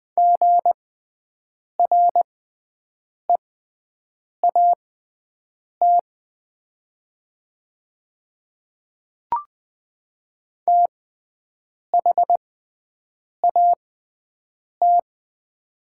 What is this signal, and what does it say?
Morse code, heard as a keyed tone, about 20 wpm, spaced as for 7 wpm, 700 Hz. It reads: GREAT THAT
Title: 6x Character Spacing - Top 300 Words - 20wpm